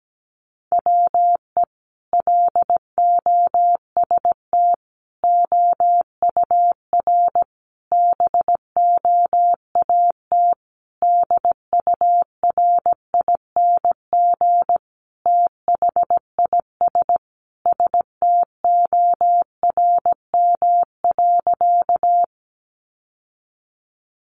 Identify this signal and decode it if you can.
Morse code, heard as a keyed tone, about 17 wpm, 700 Hz. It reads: WE LOST OUR BOAT DURING THIS STORM.
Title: Sentences from Top 1500 Words 17wpm